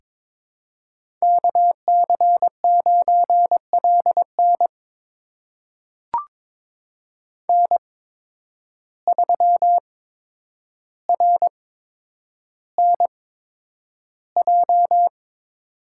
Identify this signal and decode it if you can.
Morse code, heard as a keyed tone, about 22 wpm, spaced as for 9 wpm, 700 Hz. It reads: KC9LN N3RNJ
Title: USA Call Signs - 5 Characters - 8x Character Spacing - 22wpm